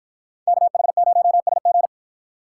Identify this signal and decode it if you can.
Morse code, heard as a keyed tone, about 53 wpm, 700 Hz. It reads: KH0SG